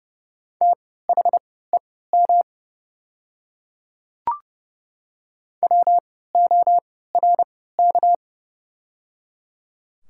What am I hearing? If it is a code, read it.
Morse code, heard as a keyed tone, about 30 wpm, spaced as for 17 wpm, 690 Hz. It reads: THEM WORK
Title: 3x Character Spacing - Top 200 Words - 30wpm